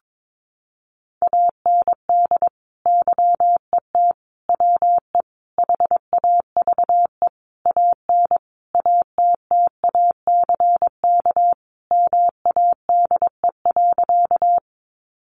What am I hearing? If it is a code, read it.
Morse code, heard as a keyed tone, about 22 wpm, 700 Hz. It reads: AND YET WE HAVE AN ATTACK MADE.
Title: Sentences from Top 1500 Words 22wpm